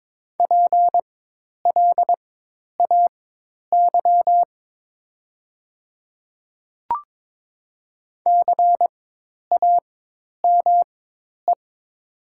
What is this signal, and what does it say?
Morse code, heard as a keyed tone, about 21 wpm, spaced as for 10 wpm, 700 Hz. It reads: PLAY CAME
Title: 4x Character Spacing - Top 300 Words - 22wpm